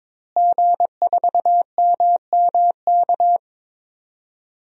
Morse code, heard as a keyed tone, about 23 wpm, 700 Hz. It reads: G4MMK
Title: DXCC-100 Call Signs - 5 Characters - Repeated 3x - 22wpm